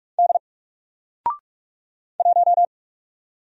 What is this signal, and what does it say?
Morse code, heard as a keyed tone, about 44 wpm, 700 Hz. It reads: D 1
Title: Single Letter-Number 45wpm